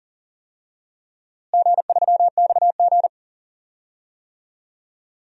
Morse code, heard as a keyed tone, about 40 wpm, 700 Hz. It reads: G3XG
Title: DXCC-100 Call Signs - 4 Characters - Repeated 3x - 40wpm